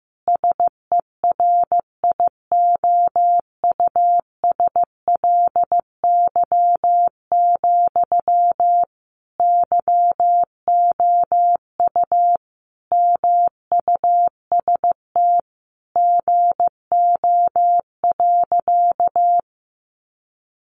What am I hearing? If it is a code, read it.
Morse code, heard as a keyed tone, about 15 wpm, 700 Hz. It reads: SERIOUSLY, YOU MUST GO.
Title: Sentences from Top 1200 Words 15wpm